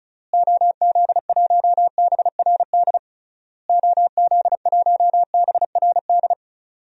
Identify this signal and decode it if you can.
Morse code, heard as a keyed tone, about 36 wpm, 700 Hz. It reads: OZ1BRD OZ1BRD
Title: DXCC-100 Call Signs - 6 Characters - Repeated 3x - 35wpm